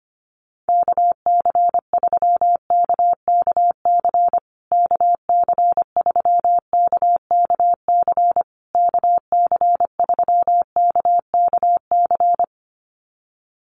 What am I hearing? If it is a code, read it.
Morse code, heard as a keyed tone, about 25 wpm, 700 Hz. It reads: KC3KKC KC3KKC KC3KKC
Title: USA Call Signs Repeated 3x 25wpm